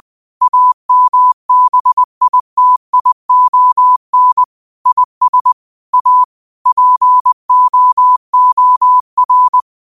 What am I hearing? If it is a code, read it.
Morse code, heard as a keyed tone, about 20 wpm, 1000 Hz. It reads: AMBITION IS A POOR